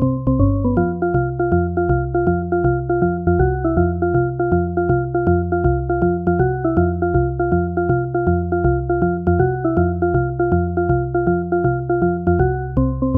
There is no code.